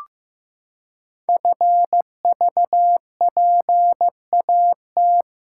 Morse code, heard as a keyed tone, about 15 wpm, 700 Hz. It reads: FVPAT
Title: Code Groups - 5 Letters - 15wpm